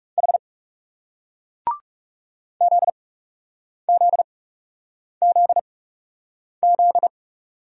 Morse code, heard as a keyed tone, about 40 wpm, spaced as for 8 wpm, 700 Hz. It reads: H ZZZZ